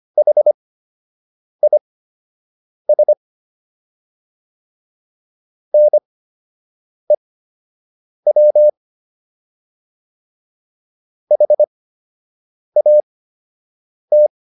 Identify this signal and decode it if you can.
Morse code, heard as a keyed tone, about 25 wpm, spaced as for 7 wpm, 600 Hz. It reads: HIS NEW HAT